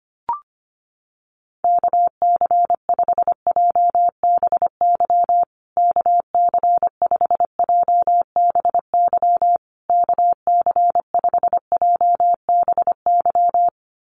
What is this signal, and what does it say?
Morse code, heard as a keyed tone, about 25 wpm, 700 Hz. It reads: KC5JBY KC5JBY KC5JBY
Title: USA Call Signs Repeated 3x 25wpm